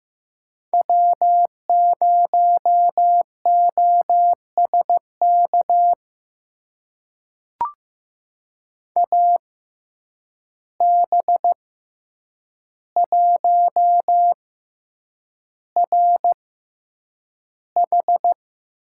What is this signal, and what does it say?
Morse code, heard as a keyed tone, about 15 wpm, spaced as for 8 wpm, 700 Hz. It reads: W0OSK AB1RH
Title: USA Call Signs - 5 Characters - 6x Character Spacing - 15wpm